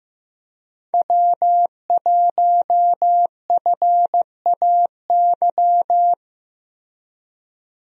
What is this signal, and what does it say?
Morse code, heard as a keyed tone, about 15 wpm, 700 Hz. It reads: W1FAY
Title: USA Call Signs - 5 Characters - Repeated 2x - 15wpm